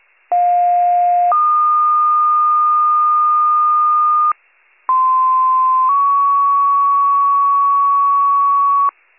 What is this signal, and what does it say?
Two-tone page, heard as a one-second tone, then a three-second tone, about 710 then 1220 Hz, about 1040 then 1120 Hz. It is heard from 0.3 to 8.9 s.